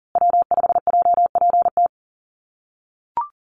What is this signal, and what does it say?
Morse code, heard as a keyed tone, about 39 wpm, 700 Hz. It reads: W5JPT
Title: USA Call Signs Repeated 2x 40wpm